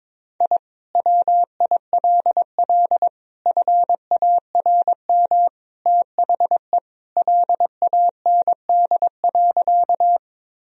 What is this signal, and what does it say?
Morse code, heard as a keyed tone, about 22 wpm, 700 Hz. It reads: I WILL FARM THE LAND.
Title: Sentences from Top 1000 Words - Review Part 2 - 22wpm